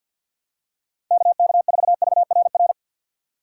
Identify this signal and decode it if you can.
Morse code, heard as a keyed tone, about 50 wpm, 700 Hz. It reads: KK4VRR
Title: USA Call Signs - 6 Characters - 3x Character Spacing - 50wpm